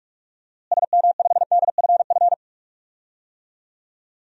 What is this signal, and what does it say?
Morse code, heard as a keyed tone, about 41 wpm, 700 Hz. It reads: SM5DFF
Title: DXCC-100 Call Signs - 6 Characters - Repeated 2x - 45wpm